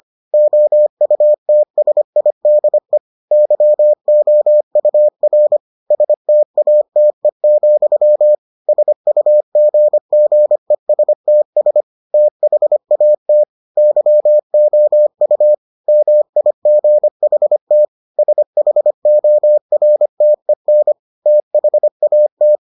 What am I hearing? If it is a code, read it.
Morse code, heard as a keyed tone, about 25 wpm, 600 Hz. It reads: OUTSIDE YOUR STATE, SUGGESTS THAT YOU MIGHT SHORTEN THAT